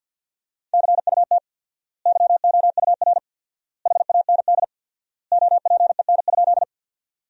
Morse code, heard as a keyed tone, about 50 wpm, 700 Hz. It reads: CUT YOUR HAND OPEN?
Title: 4x Word-Spacing - Sentences from Top 300 Words - 50wpm